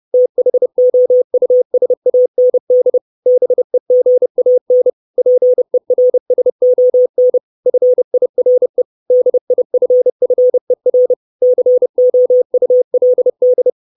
Morse code, heard as a keyed tone, about 30 wpm, 500 Hz. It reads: THOUSAND BEGAN PERSON FIRE DIFFER COULD
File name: 632 Words 30WPM